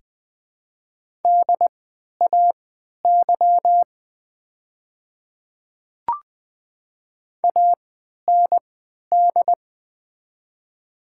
Morse code, heard as a keyed tone, about 20 wpm, spaced as for 11 wpm, 700 Hz. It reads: DAY AND
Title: 3x Character Spacing - Top 100 Words - 20wpm